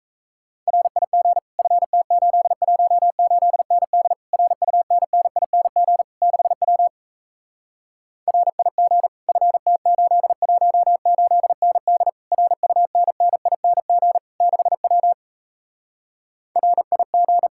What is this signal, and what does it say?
Morse code, heard as a keyed tone, about 37 wpm, 700 Hz. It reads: RIG FT818ND RUNNING 6W RIG FT818ND RUNNING 6W RIG